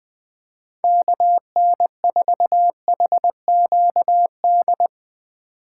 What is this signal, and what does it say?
Morse code, heard as a keyed tone, about 20 wpm, 700 Hz. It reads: KN4HQD
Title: Speed Racing - USA 2x3 Call Signs 20wpm